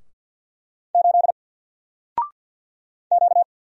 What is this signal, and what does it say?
Morse code, heard as a keyed tone, about 49 wpm, 700 Hz. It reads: 8 Q
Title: Single Letter-Number Alt 50wpm